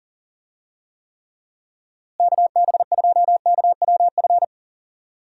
Morse code, heard as a keyed tone, about 37 wpm, 700 Hz. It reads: KB2KWF
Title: USA Call Signs - 6 Characters - Repeated 2x - 40wpm